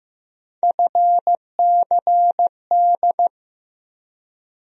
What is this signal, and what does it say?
Morse code, heard as a keyed tone, about 15 wpm, 700 Hz. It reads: FCD
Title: USA Call Sign Postfixes - 3 Letters 15wpm